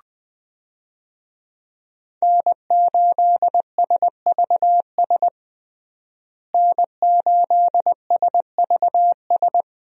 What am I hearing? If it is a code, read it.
Morse code, heard as a keyed tone, about 20 wpm, 700 Hz. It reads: N8SVS N8SVS
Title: USA Call Signs - 5 Characters - Repeated 2x - 20wpm